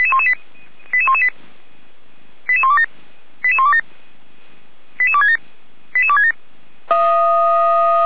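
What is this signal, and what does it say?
Five-tone selective call (ZVEI), heard as quick runs of five tones, tones of about 80 ms. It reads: 88118 88127 88127 88267 88267
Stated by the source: ZVEI five-tone call received by SDR.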